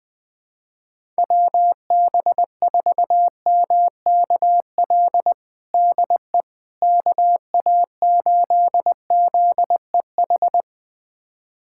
Morse code, heard as a keyed tone, about 20 wpm, 700 Hz. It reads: WB4MKL DE KA8ZEH
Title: USA Callsign DE Callsign - 6 Characters - 20wpm